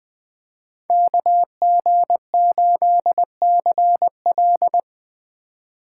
Morse code, heard as a keyed tone, about 21 wpm, 700 Hz. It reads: KG8CL
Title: Speed Racing - USA 2x2 Call Signs 17wpm